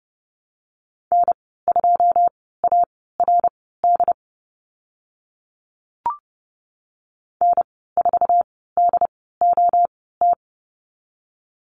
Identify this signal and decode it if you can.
Morse code, heard as a keyed tone, about 30 wpm, spaced as for 17 wpm, 700 Hz. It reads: N2ARD N4DOT
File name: Lesson 14g - Number 2 - Call Signs 30wpm